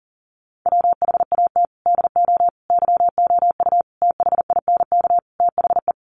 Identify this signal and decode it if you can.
Morse code, heard as a keyed tone, about 37 wpm, 700 Hz. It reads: WHAT DO YOU THINK THE
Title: Sets of 5 Words Encore 40wpm